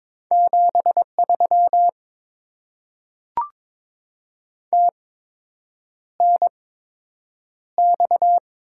Morse code, heard as a keyed tone, about 22 wpm, spaced as for 8 wpm, 700 Hz. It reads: 73 TNX
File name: Top 20 QSO Elements - 8x Character Spacing - 22wpm